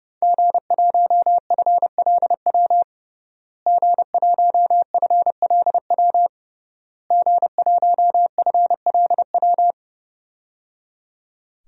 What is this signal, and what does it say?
Morse code, heard as a keyed tone, about 31 wpm, 700 Hz. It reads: G1FLW G1FLW G1FLW
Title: DXCC-100 Call Signs - 5 Characters - Repeated 3x - 30wpm